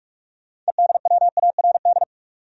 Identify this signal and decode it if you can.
Morse code, heard as a keyed tone, about 46 wpm, 700 Hz. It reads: EDWARD
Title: USA Common Names - 45wpm